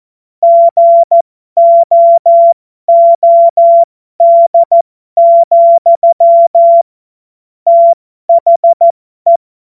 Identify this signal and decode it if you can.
Morse code, heard as a keyed tone, about 14 wpm, spaced as for 12 wpm, 680 Hz. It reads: GOOD, THE
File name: Morsekurs / Witze